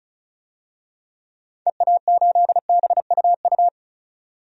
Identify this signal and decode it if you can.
Morse code, heard as a keyed tone, about 36 wpm, 700 Hz. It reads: EA8BUU
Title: DXCC-100 Call Signs - 6 Characters - 35wpm